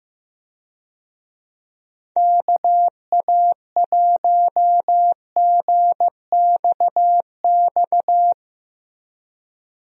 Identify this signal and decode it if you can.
Morse code, heard as a keyed tone, about 15 wpm, 700 Hz. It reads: KA1GXX
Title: USA Call Signs - 6 Characters - Repeated 2x - 15wpm